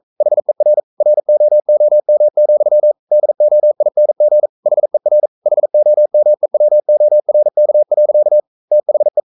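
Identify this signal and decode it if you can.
Morse code, heard as a keyed tone, about 42 wpm, 600 Hz. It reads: HER ROOM, DOING HER HOMEWORK. THE
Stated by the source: Morse code sending of an English novel at 42 wpm.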